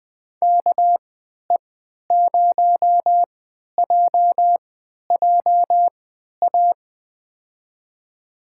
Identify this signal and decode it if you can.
Morse code, heard as a keyed tone, about 21 wpm, spaced as for 11 wpm, 700 Hz. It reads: KE0JJA
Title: USA Call Signs - 6 Characters - 3x Character Spacing - 20wpm